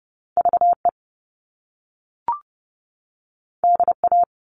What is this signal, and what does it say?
Morse code, heard as a keyed tone, about 30 wpm, 700 Hz. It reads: VE DA
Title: All Cut Numbers - 2-Digit 30wpm